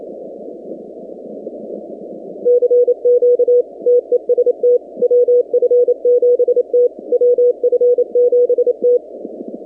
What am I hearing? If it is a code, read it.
Morse code, heard as a keyed tone, about 28 wpm, 490 Hz. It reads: CQ TEST WF7T WF7T
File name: 2019-cw-wpx-kt5j-k5pi